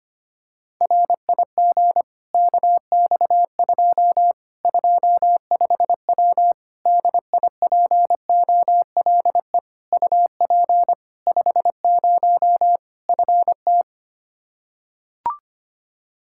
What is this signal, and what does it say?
Morse code, heard as a keyed tone, about 25 wpm, 700 Hz. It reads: RIG KX2 25W DIPOLE UP 50 FT
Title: Practice QSO - Exchange 2 - Info - 25wpm